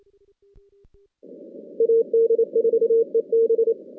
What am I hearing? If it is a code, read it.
Morse code, heard as a keyed tone, about 28 wpm, 460 Hz. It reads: AD4EB
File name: 2016 JAN CW NAQP / 2016 Jan naqp k5ot - 0-10